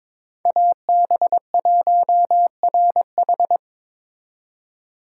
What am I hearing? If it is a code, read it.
Morse code, heard as a keyed tone, about 22 wpm, 700 Hz. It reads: AB1RH
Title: USA Call Signs - 5 Characters - 3x Character Spacing - 22wpm